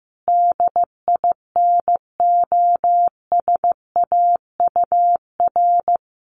Morse code, heard as a keyed tone, about 15 wpm, 700 Hz. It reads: DINOSAUR
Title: A Sound Of Thunder 15wpm